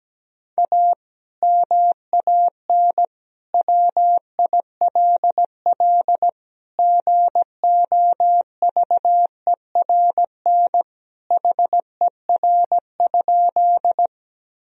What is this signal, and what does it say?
Morse code, heard as a keyed tone, about 17 wpm, 700 Hz. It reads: A MAN WILL GOVERN HER?
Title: Sentences from Top 1000 Words - Review Part 2 - 17wpm